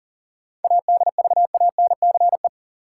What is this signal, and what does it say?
Morse code, heard as a keyed tone, about 37 wpm, 700 Hz. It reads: ADVANCE
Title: Sentences from Top 1400 Words - New Word Drill - 40wpm